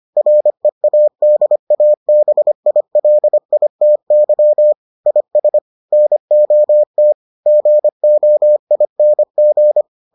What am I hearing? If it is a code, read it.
Morse code, heard as a keyed tone, about 25 wpm, 600 Hz. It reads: READABILITY IS NOT GOING